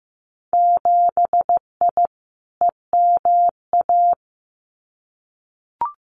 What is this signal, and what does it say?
Morse code, heard as a keyed tone, about 15 wpm, 700 Hz. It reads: 7I EMA
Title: Winter Field Day Exchanges Encore 15wpm